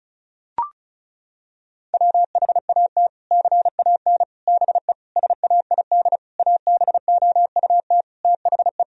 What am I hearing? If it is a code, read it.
Morse code, heard as a keyed tone, about 34 wpm, 700 Hz. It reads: WHAT CAN BE SAID ABOUT THE
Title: Sentences from Top 1000 Words - Review Part 4 - 35wpm